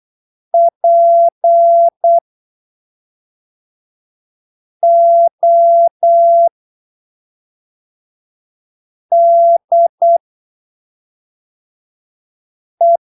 Morse code, heard as a keyed tone, about 8 wpm, spaced as for 3 wpm, 680 Hz. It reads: PODE
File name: arquivos / texto / texto 18 08ppm